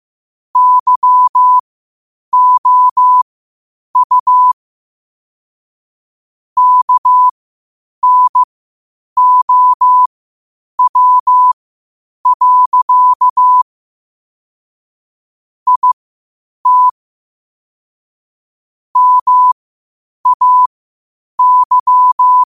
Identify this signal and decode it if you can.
Morse code, heard as a keyed tone, about 15 wpm, spaced as for 8 wpm, 1000 Hz. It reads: YOU KNOW. IT MAY
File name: qotdfiles / 8wpm